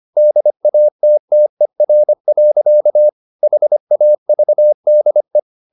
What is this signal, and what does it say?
Morse code, heard as a keyed tone, about 25 wpm, 600 Hz. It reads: DATTER. HAVDE
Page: Ferie 25 wpm